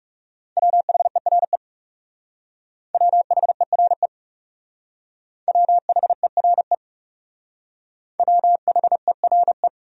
Morse code, heard as a keyed tone, about 39 wpm, 700 Hz. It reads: WHERE WHERE WHERE WHERE